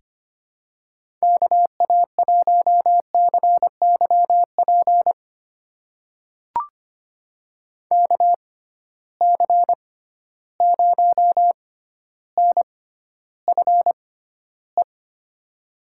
Morse code, heard as a keyed tone, about 25 wpm, spaced as for 13 wpm, 700 Hz. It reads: KA1CYP KC0NFE
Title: USA Call Signs - 6 Characters - 6x Character Spacing - 25wpm